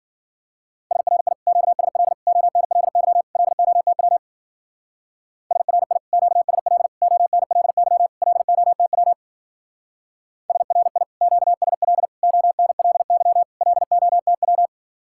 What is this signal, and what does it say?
Morse code, heard as a keyed tone, about 49 wpm, 700 Hz. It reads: SRI QSL ONLY LOTW SRI QSL ONLY LOTW SRI QSL ONLY LOTW
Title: Speed-Racing Common QSO Phrases - 40wpm